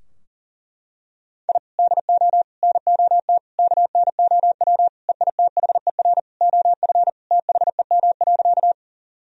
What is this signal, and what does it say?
Morse code, heard as a keyed tone, about 40 wpm, 690 Hz. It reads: I DO NOT KNOW EITHER OF THEM.
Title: Sentences from Top 1000 Words - Review Part 4 - 40wpm